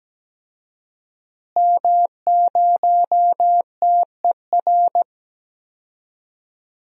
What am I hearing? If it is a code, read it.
Morse code, heard as a keyed tone, about 17 wpm, 700 Hz. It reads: M0TER
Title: DXCC-100 Call Signs - 5 Characters - Repeated 2x - 17wpm